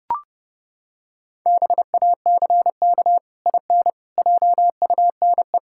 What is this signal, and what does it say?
Morse code, heard as a keyed tone, about 30 wpm, 700 Hz. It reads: BACK IN JUNE
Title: Pangram Challenge v2 - 30wpm